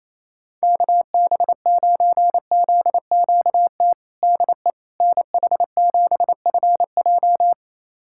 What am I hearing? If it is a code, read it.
Morse code, heard as a keyed tone, about 28 wpm, 700 Hz. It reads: KB9ZQT DE NH7FJ
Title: USA Callsign DE Callsign 28wpm